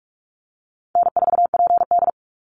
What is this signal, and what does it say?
Morse code, heard as a keyed tone, about 44 wpm, 700 Hz. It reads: N4PD
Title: Easy Call Signs - 45wpm